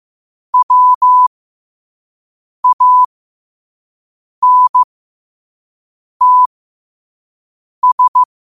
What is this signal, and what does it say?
Morse code, heard as a keyed tone, about 15 wpm, spaced as for 5 wpm, 1000 Hz. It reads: WANTS